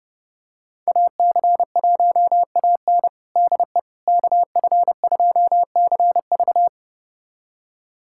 Morse code, heard as a keyed tone, about 30 wpm, 700 Hz. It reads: AC1AN DE KF2CV
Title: USA Callsign DE Callsign - 5 Characters - 30wpm